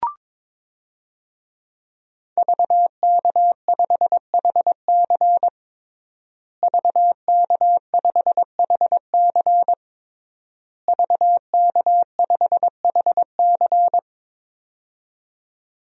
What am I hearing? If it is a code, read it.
Morse code, heard as a keyed tone, about 22 wpm, 700 Hz. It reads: VK5HC VK5HC VK5HC
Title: DXCC-100 Call Signs - 5 Characters - Repeated 3x - 22wpm